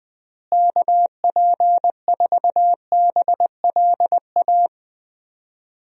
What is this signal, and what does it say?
Morse code, heard as a keyed tone, about 21 wpm, 700 Hz. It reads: KP4BLA
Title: DXCC-100 Call Signs - 6 Characters - Repeated 3x - 20wpm